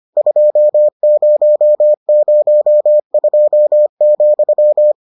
Morse code, heard as a keyed tone, about 25 wpm, 600 Hz. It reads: 2002,